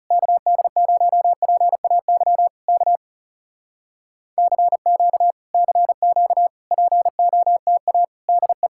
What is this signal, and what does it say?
Morse code, heard as a keyed tone, about 37 wpm, 700 Hz. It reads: KD0PAY K CQ CQ POTA DE
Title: Speed-Racing Common QSO Phrases - 30wpm